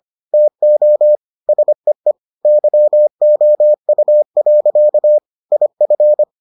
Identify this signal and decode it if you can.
Morse code, heard as a keyed tone, about 25 wpm, 600 Hz. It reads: TO SEE YOU. IF